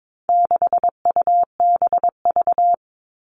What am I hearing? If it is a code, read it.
Morse code, heard as a keyed tone, about 22 wpm, 700 Hz. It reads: 6UBV
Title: All Cut Numbers - 4-Digit 22wpm